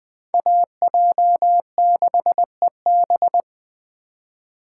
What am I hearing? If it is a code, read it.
Morse code, heard as a keyed tone, about 20 wpm, 700 Hz. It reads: AJ6EB